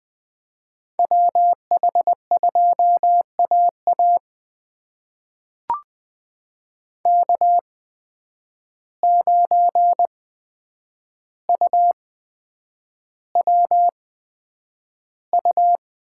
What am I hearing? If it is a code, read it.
Morse code, heard as a keyed tone, about 20 wpm, spaced as for 9 wpm, 700 Hz. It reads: WH2AA K9UWU
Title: USA Call Signs - 5 Characters - 8x Character Spacing - 20wpm